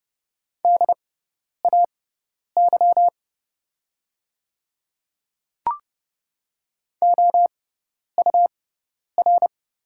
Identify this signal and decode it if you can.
Morse code, heard as a keyed tone, about 30 wpm, spaced as for 10 wpm, 700 Hz. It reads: DAY OUR
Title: 6x Character Spacing - Top 200 Words - 30wpm